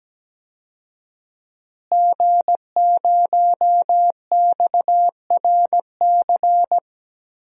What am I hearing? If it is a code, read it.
Morse code, heard as a keyed tone, about 17 wpm, 700 Hz. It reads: G0XRC
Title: DXCC-100 Call Signs - 5 Characters - Repeated 2x - 17wpm